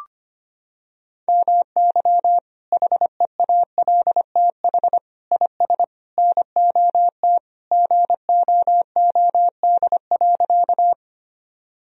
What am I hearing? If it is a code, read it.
Morse code, heard as a keyed tone, about 25 wpm, 700 Hz. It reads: MY HEALTH IS NOT GOOD.
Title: Sentences from Top 1000 Words - Review Part 3 - 25wpm